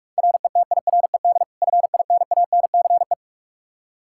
Morse code, heard as a keyed tone, about 47 wpm, 700 Hz. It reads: RETIRED FINANCE
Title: Practice QSO - Exchange 3 - More Info - 45wpm